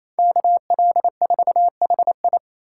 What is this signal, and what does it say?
Morse code, heard as a keyed tone, about 28 wpm, 700 Hz. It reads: KL4HI
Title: USA Call Signs - 5 Characters - 4x Character Spacing - 28wpm